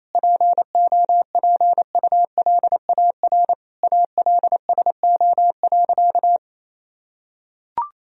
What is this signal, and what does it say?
Morse code, heard as a keyed tone, about 28 wpm, 700 Hz. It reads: POPULAR ALSO.
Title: Sentences from Top 1000 Words - Review Part 3 - 28wpm